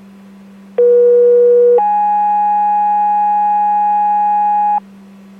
Two-tone page, heard as a one-second tone, then a three-second tone, about 480 then 830 Hz, with noise. It is heard from 0.8 to 4.8 s.